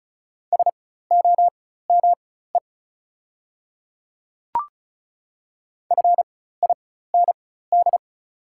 Morse code, heard as a keyed tone, about 36 wpm, spaced as for 16 wpm, 700 Hz. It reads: SOME FIND
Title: 4x Character Spacing - Top 100 Words - 35wpm